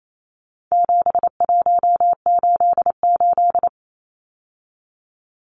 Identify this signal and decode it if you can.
Morse code, heard as a keyed tone, about 28 wpm, 700 Hz. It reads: 7188
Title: Numbers - 4-Digit 28wpm